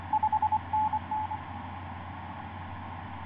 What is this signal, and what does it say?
Morse code, heard as a keyed tone, about 24 wpm, 860 Hz. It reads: 5NN
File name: 5i3a ik2fiq etc 12m cw 09feb2010 1151